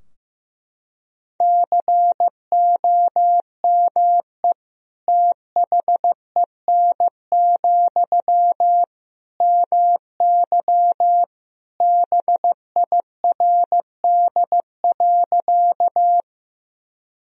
Morse code, heard as a keyed tone, about 15 wpm, 700 Hz. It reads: COME THEN, MY BIRD.
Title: Sentences from Top 1000 Words - Review Part 2 - 15wpm